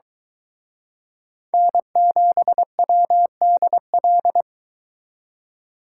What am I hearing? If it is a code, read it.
Morse code, heard as a keyed tone, about 24 wpm, 700 Hz. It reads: N7WDL